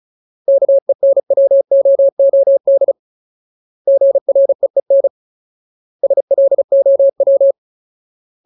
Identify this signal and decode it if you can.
Morse code, heard as a keyed tone, about 34 wpm, 550 Hz. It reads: KENWOOD GREEN SLOW
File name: Ultimate Word List / 35WPM Ultimate